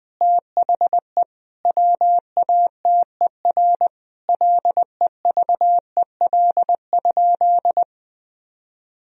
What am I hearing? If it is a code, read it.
Morse code, heard as a keyed tone, about 21 wpm, 700 Hz. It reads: THE WATER LEVEL?
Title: Sentences from Top 1000 Words - Review Part 4 - 20wpm